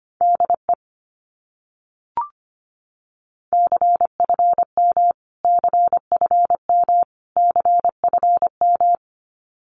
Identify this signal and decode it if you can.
Morse code, heard as a keyed tone, about 25 wpm, 700 Hz. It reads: DE CFM CFM CFM
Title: Abbreviations Repeated 3x - 25wpm